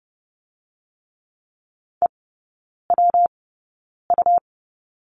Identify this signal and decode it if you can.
Morse code, heard as a keyed tone, about 30 wpm, spaced as for 9 wpm, 700 Hz. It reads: EWU
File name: Lesson 19e - Letter w - Three Characters 30wpm